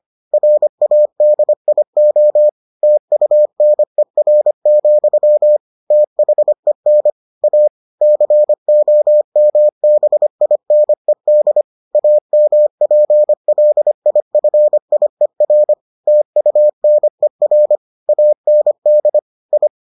Morse code, heard as a keyed tone, about 25 wpm, 600 Hz. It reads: RADIO TUNER, THEN A COMBINED AMPLIFIER TUNER AND I